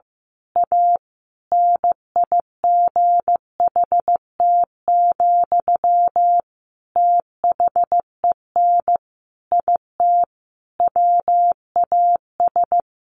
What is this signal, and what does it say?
Morse code, heard as a keyed tone, about 15 wpm, 700 Hz. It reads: A NIGHT, THEN IT WAS
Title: A Sound Of Thunder 15wpm